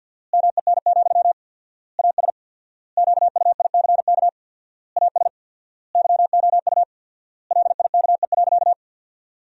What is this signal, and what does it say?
Morse code, heard as a keyed tone, about 53 wpm, 700 Hz. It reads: MEN, AS QUICK AS YOU LIKE.
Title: 4x Word-Spacing - Sentences from Top 500 Words - 50wpm